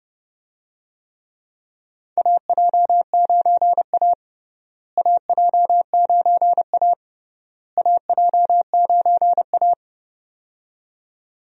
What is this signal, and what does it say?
Morse code, heard as a keyed tone, about 30 wpm, 700 Hz. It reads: AJ9A AJ9A AJ9A
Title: USA Call Signs - 4 Characters - Repeated 3x - 30wpm